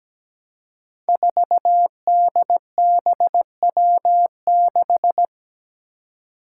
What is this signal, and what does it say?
Morse code, heard as a keyed tone, about 17 wpm, 700 Hz. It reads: 4DBW6
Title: Five Challenging Letter-Number Combinations 17wpm